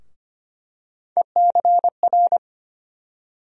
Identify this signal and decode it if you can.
Morse code, heard as a keyed tone, about 25 wpm, 700 Hz. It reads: ECR
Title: USA Call Sign Postfixes - 3 Letters 25wpm